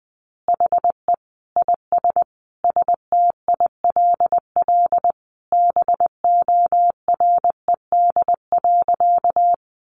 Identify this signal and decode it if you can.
Morse code, heard as a keyed tone, about 20 wpm, 700 Hz. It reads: HE IS STILL BORED.
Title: Easy Phrases 20wpm